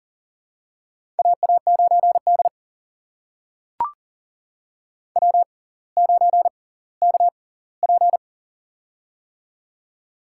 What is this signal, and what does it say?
Morse code, heard as a keyed tone, about 38 wpm, spaced as for 20 wpm, 700 Hz. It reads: AA9D W9KP
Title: USA Call Signs - 4 Characters - 6x Character Spacing - 40wpm